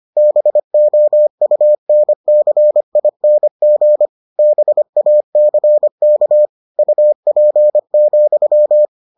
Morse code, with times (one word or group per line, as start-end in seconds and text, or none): 0.2-4.1 BOUNCING
4.4-6.5 BACK
6.8-8.8 UP,